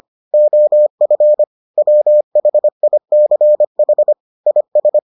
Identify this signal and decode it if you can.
Morse code, heard as a keyed tone, about 25 wpm, 600 Hz. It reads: OF WHICH IS